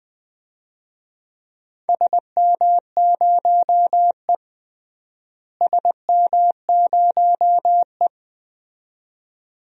Morse code, heard as a keyed tone, about 21 wpm, 700 Hz. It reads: SM0E SM0E
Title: DXCC-100 Call Signs - 4 Characters - Repeated 2x - 20wpm